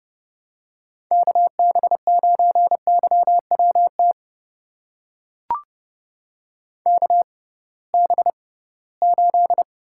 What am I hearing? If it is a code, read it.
Morse code, heard as a keyed tone, about 30 wpm, spaced as for 19 wpm, 700 Hz. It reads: KB9YWT KB8
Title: USA Call Signs - 6 Characters - 6x Character Spacing - 30wpm